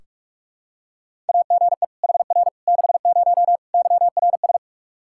Morse code, heard as a keyed tone, about 46 wpm, 700 Hz. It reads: AGE HR 60 YRS
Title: Practice QSO - Exchange 3 - More Info - 45wpm